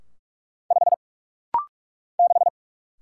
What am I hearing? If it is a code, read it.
Morse code, heard as a keyed tone, about 47 wpm, 700 Hz. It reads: 5 6
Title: Single Letter-Number - Rapid-Fire 45wpm